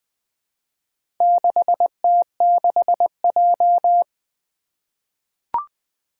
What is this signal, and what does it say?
Morse code, heard as a keyed tone, about 21 wpm, 700 Hz. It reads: 6T6J
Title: Four Challenging Letter-Number Combinations 20wpm